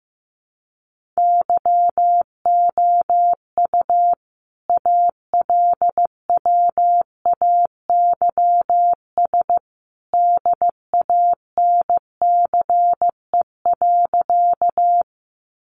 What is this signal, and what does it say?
Morse code, heard as a keyed tone, about 15 wpm, 700 Hz. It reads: YOU ALWAYS DANCE.
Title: Sentences from Top 700 Words 15wpm